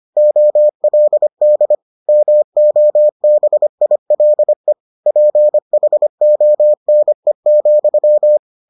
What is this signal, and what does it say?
Morse code, heard as a keyed tone, about 25 wpm, 600 Hz. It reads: OLD MOBILE PHONE,